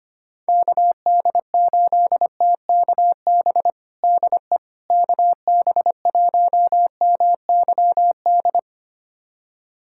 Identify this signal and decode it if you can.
Morse code, heard as a keyed tone, about 25 wpm, 700 Hz. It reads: KD8TKB DE KB1MYD
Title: USA Callsign DE Callsign - 6 Characters - 25wpm